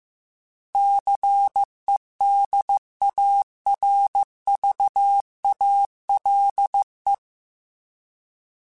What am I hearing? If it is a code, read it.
Morse code, heard as a keyed tone, about 15 wpm, 800 Hz. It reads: CEDARVALE
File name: fromSMs / FromSMs mp3s / Cities@13